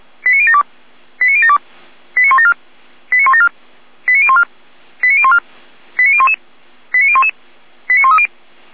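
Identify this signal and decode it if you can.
Five-tone selective call (ZVEI), heard as quick runs of five tones, tones of about 70 ms. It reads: 89072 89072 89174 89174 89014 89014 89011 89011 89130
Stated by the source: ZVEI five-tone selective call received on SDR.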